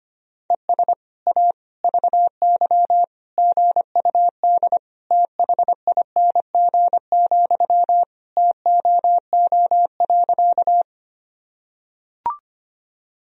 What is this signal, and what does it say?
Morse code, heard as a keyed tone, about 25 wpm, 700 Hz. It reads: ES A VY GUD THING, TOO.
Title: Sentences from Top 200 Words and Abbreviations 25wpm